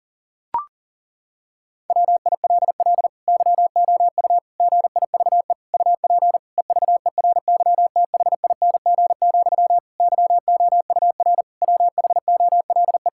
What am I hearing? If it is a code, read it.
Morse code, heard as a keyed tone, about 40 wpm, 700 Hz. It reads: WILL YOU GIVE UP EVERYTHING, YOUR WHOLE
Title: Sentences from Top 1000 Words - Review Part 5 - 40wpm